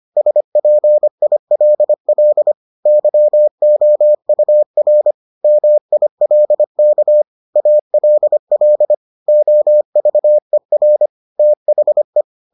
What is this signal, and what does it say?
Morse code, heard as a keyed tone, about 25 wpm, 600 Hz. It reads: SPILL YOUR MILK ALL OVER THE